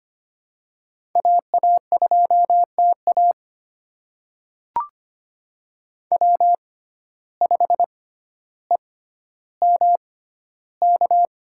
Morse code, heard as a keyed tone, about 25 wpm, spaced as for 13 wpm, 690 Hz. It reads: AA2TA W5EMK
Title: USA Call Signs - 5 Characters - 6x Character Spacing - 25wpm